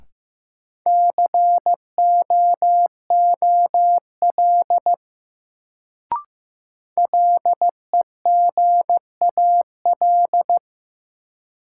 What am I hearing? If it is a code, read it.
Morse code, heard as a keyed tone, about 15 wpm, 700 Hz. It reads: COOL LEGAL
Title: Sentences from Top 1200 Words - New Word Drill Alt - 15wpm